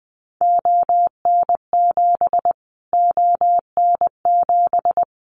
Morse code, heard as a keyed tone, about 20 wpm, 700 Hz. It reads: ON7 ON7
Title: Callsign Prefixes for Top 25 DXCC - Repeated 3x - 20wpm